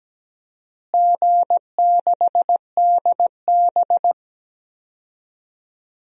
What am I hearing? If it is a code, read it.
Morse code, heard as a keyed tone, about 17 wpm, 700 Hz. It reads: G6DB
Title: DXCC-100 Call Signs - 4 Characters - Repeated 2x - 17wpm